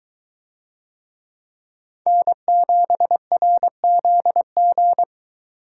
Morse code, heard as a keyed tone, about 21 wpm, 700 Hz. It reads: N7RZG